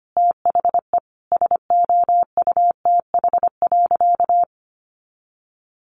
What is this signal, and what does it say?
Morse code, heard as a keyed tone, about 25 wpm, 700 Hz. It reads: THE SOUTH.
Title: Sentences from Top 400 Words 25wpm